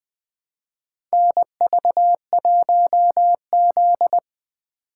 Morse code, heard as a keyed tone, about 20 wpm, 700 Hz. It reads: NV1Z
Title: USA Call Signs - 4 Characters - 3x Character Spacing - 20wpm